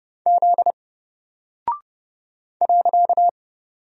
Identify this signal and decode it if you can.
Morse code, heard as a keyed tone, about 30 wpm, 700 Hz. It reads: Z .
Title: Single Character (Letter:Number:Punctuation:Prosign) 30wpm